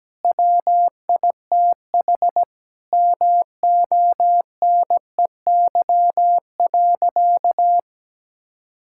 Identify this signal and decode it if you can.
Morse code, heard as a keyed tone, about 17 wpm, 700 Hz. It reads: WITH MONEY.
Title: Sentences from Top 1600 Words 17wpm